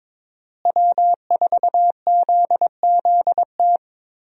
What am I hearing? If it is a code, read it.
Morse code, heard as a keyed tone, about 22 wpm, 700 Hz. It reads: W4ZZT
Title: USA Call Signs - 5 Characters - Repeated 3x - 22wpm